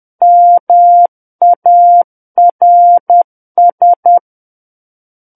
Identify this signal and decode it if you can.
Morse code, heard as a keyed tone, about 10 wpm, 700 Hz. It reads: MARS
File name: Book One / 01-Chapter 1 - 10 WPM